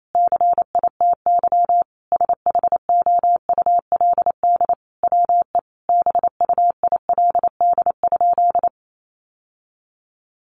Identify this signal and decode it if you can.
Morse code, heard as a keyed tone, about 28 wpm, 700 Hz. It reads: CITY SHOULD WE BUILD?
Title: Sentences from Top 1300 Words 28wpm